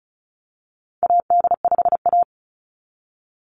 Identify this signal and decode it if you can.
Morse code, heard as a keyed tone, about 34 wpm, 700 Hz. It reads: AD5A
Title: NA SOTA Activator Callsigns 35wpm